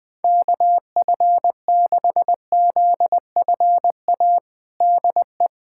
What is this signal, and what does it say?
Morse code, heard as a keyed tone, about 21 wpm, 700 Hz. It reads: KF6ZFA DE
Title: USA Callsign DE Callsign - 6 Characters - 20wpm